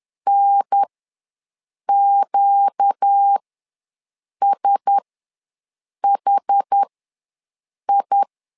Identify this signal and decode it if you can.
Morse code, heard as a keyed tone, about 10 wpm, spaced as for 6 wpm, 800 Hz. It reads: NQSHI
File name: LEZIONI / 4 Q1S ws 10wpm 2208